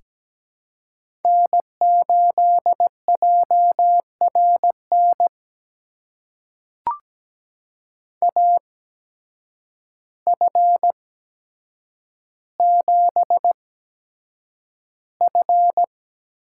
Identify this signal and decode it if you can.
Morse code, heard as a keyed tone, about 17 wpm, spaced as for 8 wpm, 700 Hz. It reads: N8JRN AF7F
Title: USA Call Signs - 5 Characters - 8x Character Spacing - 17wpm